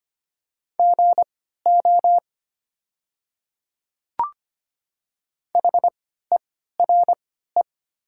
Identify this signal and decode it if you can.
Morse code, heard as a keyed tone, about 25 wpm, spaced as for 14 wpm, 700 Hz. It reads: GO HERE
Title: 3x Character Spacing - Top 500 Words - 25wpm